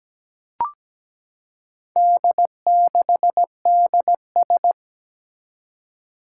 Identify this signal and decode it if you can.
Morse code, heard as a keyed tone, about 17 wpm, 700 Hz. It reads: D6DS
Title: Four Challenging Letter-Number Combinations 17wpm